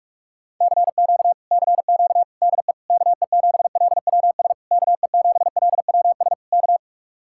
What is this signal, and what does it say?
Morse code, heard as a keyed tone, about 40 wpm, 700 Hz. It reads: CQ CQ DE KE7LWS KE7LWS K